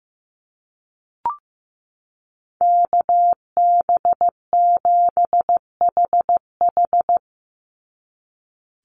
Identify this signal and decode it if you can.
Morse code, heard as a keyed tone, about 15 wpm, 700 Hz. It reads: KB7HH
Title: NA SOTA Chaser Callsigns 15wpm